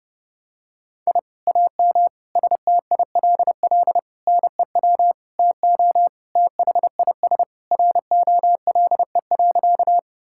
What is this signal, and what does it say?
Morse code, heard as a keyed tone, about 30 wpm, 700 Hz. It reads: I AM STILL NEW TO THIS ROLE.
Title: Sentences from Top 1000 Words - Review Part 6 - 30wpm